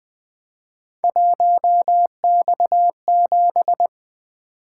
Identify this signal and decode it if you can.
Morse code, heard as a keyed tone, about 20 wpm, 700 Hz. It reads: 1X7
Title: Three Letter-Number Combinations - 20wpm